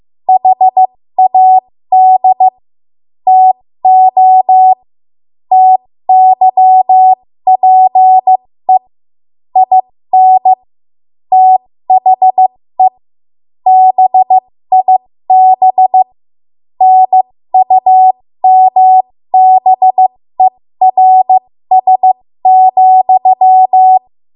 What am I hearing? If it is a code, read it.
Morse code, heard as a keyed tone, about 15 wpm, spaced as for 13 wpm, 750 Hz. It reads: HAD TO TYPE IN THE BIB NUMBERS,
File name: Morse / 13WPM